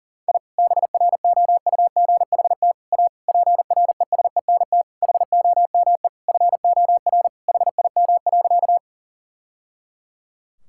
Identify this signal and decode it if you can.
Morse code, heard as a keyed tone, about 40 wpm, 700 Hz. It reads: I BROUGHT A PRESENT HOME FOR HIM.
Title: Sentences from Top 1000 Words - Review Part 5 - 40wpm